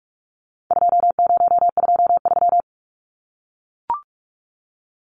Morse code, heard as a keyed tone, about 44 wpm, 700 Hz. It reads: 2023
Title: Numbers - 4-Digit 45wpm